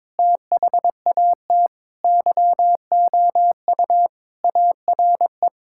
Morse code, heard as a keyed tone, about 22 wpm, 700 Hz. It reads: THAT YOU ARE
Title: Sentences from Top 1000 Words - Review Part 5 - 22wpm